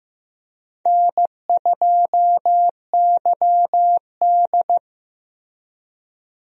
Morse code, heard as a keyed tone, about 15 wpm, 700 Hz. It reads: N2YD
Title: USA Call Signs - 4 Characters - Repeated 3x - 15wpm